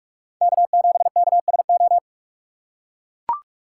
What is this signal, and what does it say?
Morse code, heard as a keyed tone, about 44 wpm, 700 Hz. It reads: K7KSO